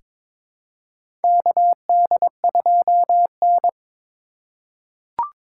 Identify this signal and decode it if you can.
Morse code, heard as a keyed tone, about 22 wpm, 700 Hz. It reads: KD2N
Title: USA Call Signs - 4x Character Spacing - 22wpm